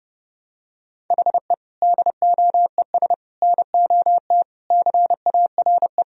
Morse code, heard as a keyed tone, about 30 wpm, 700 Hz. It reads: HE DOES NOT CARE
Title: Sentences from Top 1000 Words - Review Part 1 - 30wpm